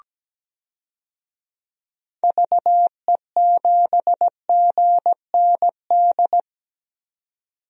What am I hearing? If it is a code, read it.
Morse code, heard as a keyed tone, about 17 wpm, 700 Hz. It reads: VE7GND
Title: DXCC-100 Call Signs - 6 Characters - 17wpm